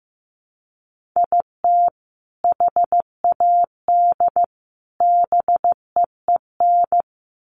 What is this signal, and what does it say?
Morse code, heard as a keyed tone, about 15 wpm, 700 Hz. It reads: IT HAD BEEN
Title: Sets of 3 Words 15wpm